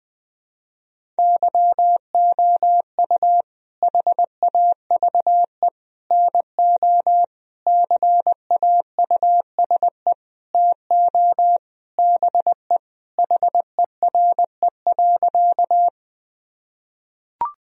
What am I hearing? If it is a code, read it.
Morse code, heard as a keyed tone, about 20 wpm, 700 Hz. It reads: YOU HAVE NO CAUSE TO BE HERE.
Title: Sentences from Top 1000 Words - Review Part 1 - 20wpm